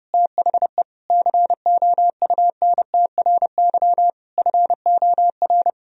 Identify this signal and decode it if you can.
Morse code, heard as a keyed tone, about 28 wpm, 700 Hz. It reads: THE COUNTRY FOR
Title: Sentences from Top 1000 Words - Review Part 1 - 30wpm